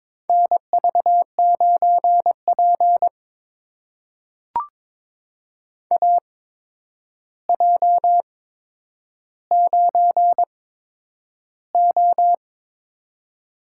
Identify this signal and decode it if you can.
Morse code, heard as a keyed tone, about 22 wpm, spaced as for 9 wpm, 700 Hz. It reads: NV9P AJ9O
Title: USA Call Signs - 4 Characters - 8x Character Spacing - 22wpm